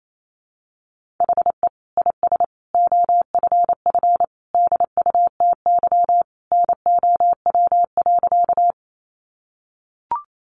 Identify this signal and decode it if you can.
Morse code, heard as a keyed tone, about 28 wpm, 700 Hz. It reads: HE IS OFF DUTY NOW.
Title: Sentences from Top 1500 Words 28wpm